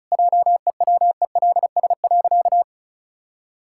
Morse code, heard as a keyed tone, about 36 wpm, 700 Hz. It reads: JEWELS.
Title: Pangram Challenge v2 - 35wpm